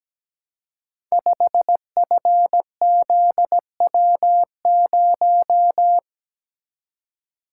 Morse code, heard as a keyed tone, about 17 wpm, 700 Hz. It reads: 5FZW0
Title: Code Groups - 5 Letter:Numbers - 17wpm